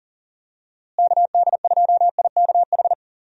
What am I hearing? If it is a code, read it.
Morse code, heard as a keyed tone, about 40 wpm, 700 Hz. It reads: KD2IKH
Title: USA Call Signs - 6 Characters - Repeated 2x - 40wpm